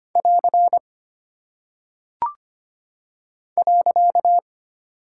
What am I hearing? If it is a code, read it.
Morse code, heard as a keyed tone, about 25 wpm, 700 Hz. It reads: <AR> .
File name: Single Punctuation-Prosign - 25wpm